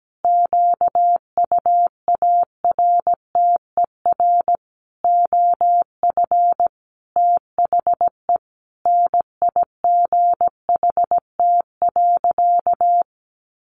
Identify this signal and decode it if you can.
Morse code, heard as a keyed tone, about 17 wpm, 700 Hz. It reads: QUARTER OF THE NIGHT.
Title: Sentences from Top 1300 Words 17wpm